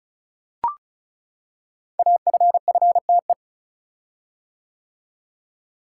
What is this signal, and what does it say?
Morse code, heard as a keyed tone, about 36 wpm, 700 Hz. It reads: AFFTE
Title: Code Groups - 5 Letters - 35wpm